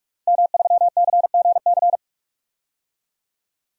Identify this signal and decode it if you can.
Morse code, heard as a keyed tone, about 41 wpm, 700 Hz. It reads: M3CGC
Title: DXCC-100 Call Signs - 5 Characters - 45wpm